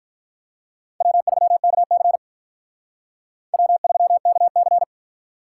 Morse code, heard as a keyed tone, about 45 wpm, 700 Hz. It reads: W3KC W3KC